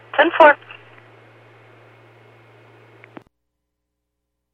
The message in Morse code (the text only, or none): none